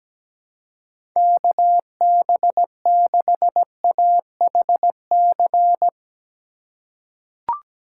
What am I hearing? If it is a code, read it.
Morse code, heard as a keyed tone, about 17 wpm, 700 Hz. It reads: KB6AHC
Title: USA Call Signs - 6 Characters - 4x Character Spacing - 17wpm